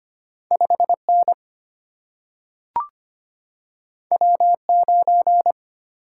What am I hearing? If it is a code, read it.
Morse code, heard as a keyed tone, about 25 wpm, 690 Hz. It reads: 5N W9
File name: Two Letter-Number Combinations - 25wpm